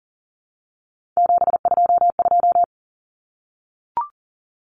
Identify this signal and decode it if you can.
Morse code, heard as a keyed tone, about 40 wpm, 700 Hz. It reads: 722
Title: Numbers - 3-Digit 40wpm